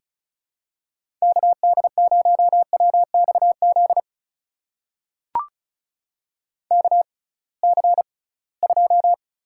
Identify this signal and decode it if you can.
Morse code, heard as a keyed tone, about 36 wpm, spaced as for 23 wpm, 700 Hz. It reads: KD0WXZ KC2
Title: USA Call Signs - 6 Characters - 6x Character Spacing - 35wpm